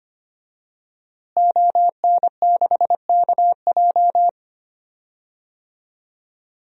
Morse code, heard as a keyed tone, about 25 wpm, 700 Hz. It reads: ON6KJ
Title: DXCC-100 Call Signs - 5 Characters - Repeated 2x - 25wpm